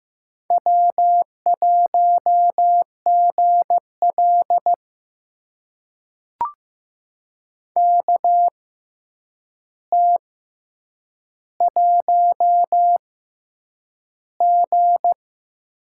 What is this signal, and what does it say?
Morse code, heard as a keyed tone, about 15 wpm, spaced as for 8 wpm, 700 Hz. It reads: W1GL KT1G
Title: USA Call Signs - 4 Characters - 6x Character Spacing - 15wpm